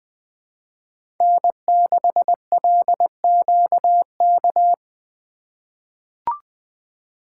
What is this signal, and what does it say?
Morse code, heard as a keyed tone, about 20 wpm, 700 Hz. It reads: N6LQK
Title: USA Call Signs - 5 Characters - 3x Character Spacing - 20wpm